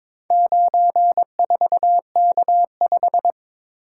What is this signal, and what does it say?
Morse code, heard as a keyed tone, about 22 wpm, 690 Hz. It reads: 94K5
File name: practice / 22wpm / Four Letter-Number Combinations - 22wpm